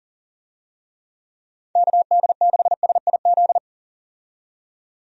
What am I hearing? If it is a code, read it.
Morse code, heard as a keyed tone, about 39 wpm, 700 Hz. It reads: KD6SIZ
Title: USA Call Signs - 6 Characters - 40wpm